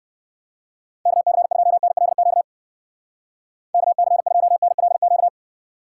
Morse code, heard as a keyed tone, about 60 wpm, 700 Hz. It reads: KC2NLQ KC2NLQ